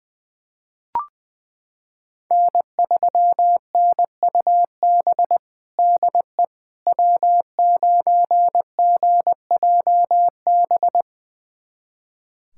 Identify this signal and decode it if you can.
Morse code, heard as a keyed tone, about 20 wpm, 700 Hz. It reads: N3NUB DE W9GJB
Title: USA Callsign DE Callsign - 5 Characters - 20wpm